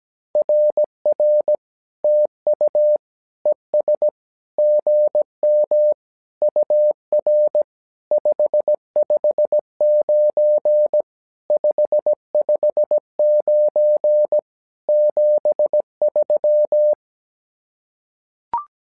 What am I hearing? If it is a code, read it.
Morse code, heard as a keyed tone, about 17 wpm, 600 Hz. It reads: RR TU ES GM UR 559 559 73
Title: SOTA QSO Practice 17wpm